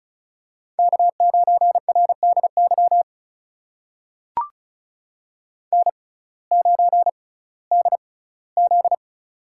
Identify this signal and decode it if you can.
Morse code, heard as a keyed tone, about 36 wpm, spaced as for 19 wpm, 700 Hz. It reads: K9RDY N9DZ
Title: USA Call Signs - 6x Character Spacing - 35wpm